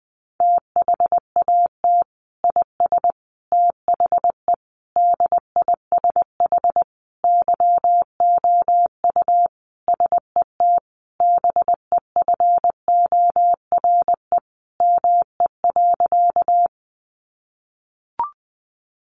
Morse code, heard as a keyed tone, about 20 wpm, 700 Hz. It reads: THAT IS THE DISH YOU SET BEFORE ME.
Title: Sentences from Top 1300 Words 20wpm